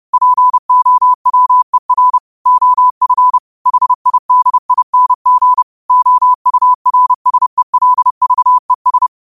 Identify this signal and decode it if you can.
Morse code, heard as a keyed tone, about 30 wpm, 1000 Hz. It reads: POWER OF HIDING OURSELVES